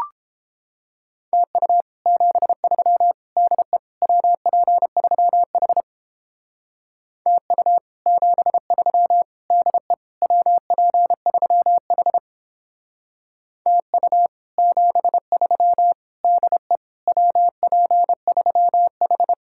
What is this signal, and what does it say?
Morse code, heard as a keyed tone, about 31 wpm, 700 Hz. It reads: TU 73 DE WP3H TU 73 DE WP3H TU 73 DE WP3H